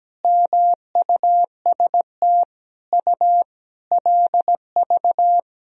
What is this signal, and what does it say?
Morse code, heard as a keyed tone, about 17 wpm, 700 Hz. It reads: MUST U LV
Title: Sentences from Top 400 Words and Abbreviations 17wpm